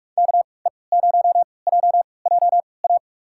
Morse code, heard as a keyed tone, about 45 wpm, spaced as for 25 wpm, 700 Hz. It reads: KE0JJA
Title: USA Call Signs - 3x Character Spacing - 45wpm